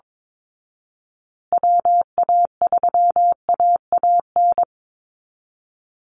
Morse code, heard as a keyed tone, about 22 wpm, 700 Hz. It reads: WA3AAN
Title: Call Signs Repeated 2x - 22wpm